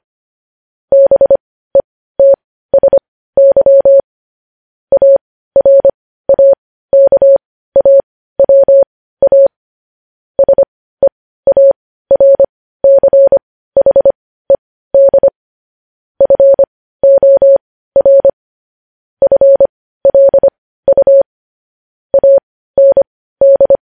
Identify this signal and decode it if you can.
Morse code, heard as a keyed tone, about 25 wpm, spaced as for 15 wpm, 550 Hz. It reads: BETSY ARAKAWA SEARCHED FOR FLU AND
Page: News Headlines in Morse Code at 15 WPM